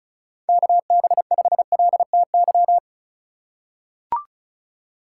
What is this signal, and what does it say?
Morse code, heard as a keyed tone, about 34 wpm, 700 Hz. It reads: KB5LTY
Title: USA Call Signs - 3x Character Spacing - 35wpm